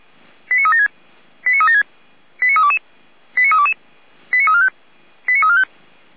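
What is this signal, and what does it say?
Five-tone selective call (ZVEI), heard as quick runs of five tones, tones of about 80 ms. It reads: 89376 89376 89322 89322 89345 89345
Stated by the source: Five-tone ZVEI selective call received by SDR, from the feuerwehr-orlamuende.org fire brigade server.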